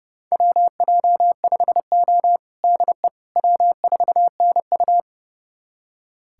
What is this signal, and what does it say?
Morse code, heard as a keyed tone, about 30 wpm, 700 Hz. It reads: WJ5O DE W4NU
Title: USA Callsign DE Callsign - 4 Characters - 30wpm